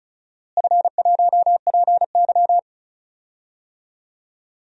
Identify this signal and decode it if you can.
Morse code, heard as a keyed tone, about 34 wpm, 700 Hz. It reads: F1PY
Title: DXCC-100 Call Signs - 4 Characters - Repeated 3x - 35wpm